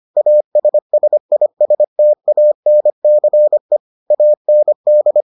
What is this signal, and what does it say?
Morse code, heard as a keyed tone, about 25 wpm, 600 Hz. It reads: ASSISTANCE AND